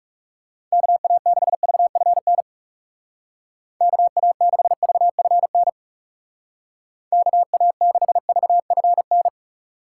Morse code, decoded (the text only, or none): KA6VFN KA6VFN KA6VFN